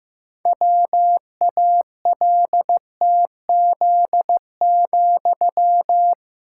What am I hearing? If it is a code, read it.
Morse code, heard as a keyed tone, about 15 wpm, 700 Hz. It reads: WALTZ,
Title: Pangram Challenge v2 - 15wpm